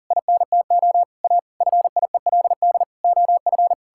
Morse code, heard as a keyed tone, about 40 wpm, 700 Hz. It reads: INTO A FIELD OF
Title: Pangram Challenge v2 - 40wpm